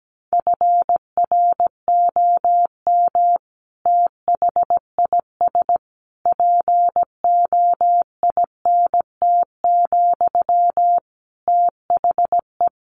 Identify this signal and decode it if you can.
Morse code, heard as a keyed tone, about 17 wpm, 700 Hz. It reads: FROM THIS POINT, THE
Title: Sentences from Top 1200 Words 17wpm